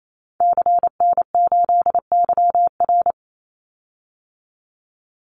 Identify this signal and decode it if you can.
Morse code, heard as a keyed tone, about 28 wpm, 700 Hz. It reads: CN8YR
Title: Call Signs Repeated 2x - 28wpm